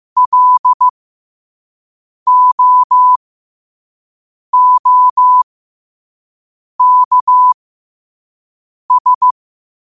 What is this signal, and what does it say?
Morse code, heard as a keyed tone, about 15 wpm, spaced as for 5 wpm, 1000 Hz. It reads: LOOKS
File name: qotdfiles / 5wpm